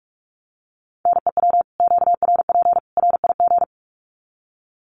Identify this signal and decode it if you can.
Morse code, heard as a keyed tone, about 44 wpm, 700 Hz. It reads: NEW QRP RIG
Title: Common QSO Phrases 45wpm